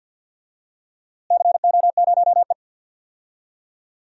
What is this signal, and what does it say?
Morse code, heard as a keyed tone, about 51 wpm, 700 Hz. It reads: CO0E
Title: DXCC-100 Call Signs - 4 Characters - Repeated 2x - 50wpm